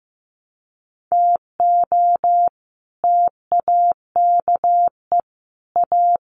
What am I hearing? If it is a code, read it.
Morse code, heard as a keyed tone, about 15 wpm, 700 Hz. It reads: TO TAKE A
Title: Sets of 5 Words Encore2 15wpm